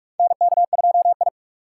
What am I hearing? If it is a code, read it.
Morse code, heard as a keyed tone, about 46 wpm, 700 Hz. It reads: NK2I
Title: USA Call Signs - 4 Characters - 2x Character Spacing - 45wpm